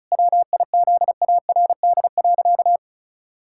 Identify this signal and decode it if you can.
Morse code, heard as a keyed tone, about 36 wpm, 700 Hz. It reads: WIZARD.
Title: Pangram Challenge v2 - 35wpm